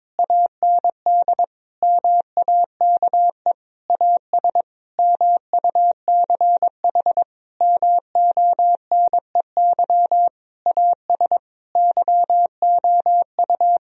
Morse code, heard as a keyed tone, about 22 wpm, 700 Hz. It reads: AND MAKE AS MUCH MONEY AS YOU
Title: Sentences from Top 1000 Words - Review Part 3 - 22wpm